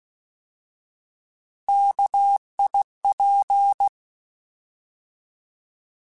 Morse code, heard as a keyed tone, about 16 wpm, 800 Hz. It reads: KIP
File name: Alphabetical / Alphabetical mp3s / Ks @15